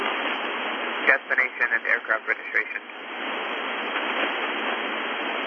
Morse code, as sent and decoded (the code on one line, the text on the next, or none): none